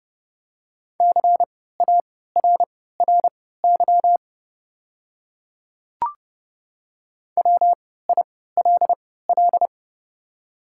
Morse code, heard as a keyed tone, about 30 wpm, spaced as for 17 wpm, 690 Hz. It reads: CARRY WILL
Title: 3x Character Spacing - Top 300 Words - 30wpm